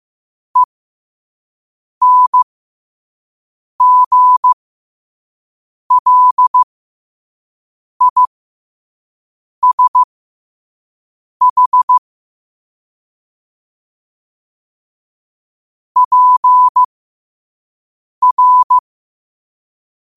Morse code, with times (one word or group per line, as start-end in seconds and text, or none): 0.6-12.0 ENGLISH
16.0-18.8 PR